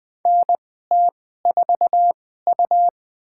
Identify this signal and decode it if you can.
Morse code, heard as a keyed tone, about 20 wpm, spaced as for 14 wpm, 700 Hz. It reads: NT4U
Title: USA Call Signs - 4 Characters - 2x Character Spacing - 20wpm